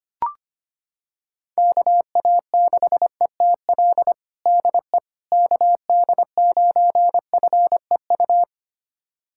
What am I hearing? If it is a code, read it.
Morse code, heard as a keyed tone, about 25 wpm, 700 Hz. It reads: KA6ETL DE KD9FEU
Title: USA Callsign DE Callsign - 6 Characters - 25wpm